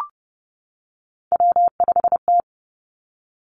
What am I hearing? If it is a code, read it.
Morse code, heard as a keyed tone, about 30 wpm, 700 Hz. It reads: W5T
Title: Three Challenging Letter-Number Combinations 30wpm